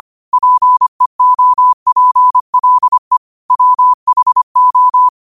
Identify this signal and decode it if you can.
Morse code, heard as a keyed tone, about 25 wpm, 1000 Hz. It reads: PEOPLE WHO
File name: qotdfiles / 25wpm